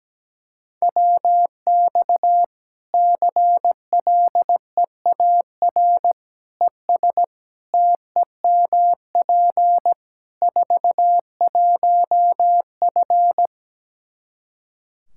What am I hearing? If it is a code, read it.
Morse code, heard as a keyed tone, about 17 wpm, 700 Hz. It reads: WX CLEAR ES TEMP 41F
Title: K1USN SST QSO Practice 17wpm